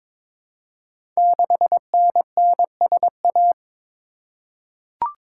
Morse code, heard as a keyed tone, about 22 wpm, 700 Hz. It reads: 6NNSA
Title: Code Groups - 5 Letter:Numbers - 22wpm